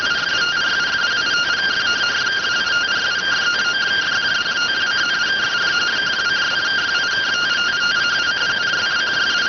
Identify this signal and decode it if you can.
Two-tone teleprinter signal, about 170 Hz shift.